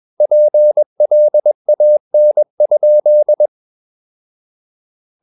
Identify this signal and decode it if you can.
Morse code, heard as a keyed tone, about 21 wpm, 600 Hz. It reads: PLAN?
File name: Ferie QRS 21wpm